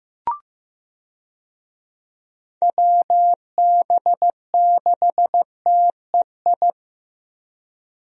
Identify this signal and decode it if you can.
Morse code, heard as a keyed tone, about 15 wpm, 700 Hz. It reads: WB6TEI
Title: USA Call Signs - 6 Characters - Repeated 2x - 15wpm